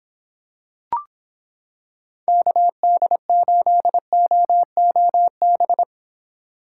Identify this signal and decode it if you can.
Morse code, heard as a keyed tone, about 26 wpm, 700 Hz. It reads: KD8OOB